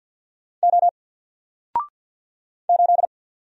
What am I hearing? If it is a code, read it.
Morse code, heard as a keyed tone, about 49 wpm, 700 Hz. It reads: O 8
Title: Single Character (Letter:Number:Punctuation:Prosign) 50wpm